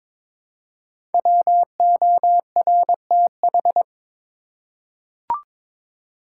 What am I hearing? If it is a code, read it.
Morse code, heard as a keyed tone, about 22 wpm, 690 Hz. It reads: WORTH
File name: 5 Letter Words - First 3 Letters Given - 22wpm